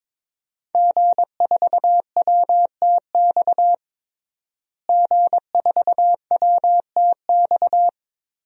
Morse code, heard as a keyed tone, about 22 wpm, 700 Hz. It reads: G4WTX G4WTX
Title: DXCC-100 Call Signs - 5 Characters - Repeated 3x - 22wpm